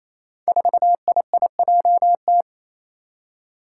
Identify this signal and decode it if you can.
Morse code, heard as a keyed tone, about 28 wpm, 700 Hz. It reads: 4IIJT
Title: Five Challenging Letter-Number Combinations 28wpm